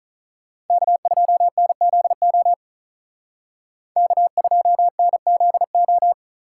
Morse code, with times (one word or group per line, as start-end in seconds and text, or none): 0.7-2.5 K2NZO
4.0-6.1 K2NZO